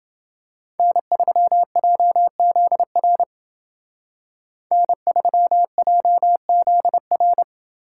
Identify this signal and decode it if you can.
Morse code, heard as a keyed tone, about 28 wpm, 700 Hz. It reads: N3JZR N3JZR